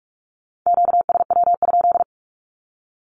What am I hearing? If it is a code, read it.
Morse code, heard as a keyed tone, about 46 wpm, 700 Hz. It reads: QSW?
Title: Q-Codes with Spoken Meaning - 45wpm